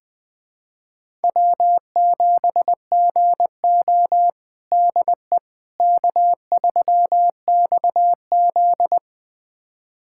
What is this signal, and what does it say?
Morse code, heard as a keyed tone, about 21 wpm, 700 Hz. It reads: W7GO DE K3XZ
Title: USA Callsign DE Callsign - 4 Characters - 20wpm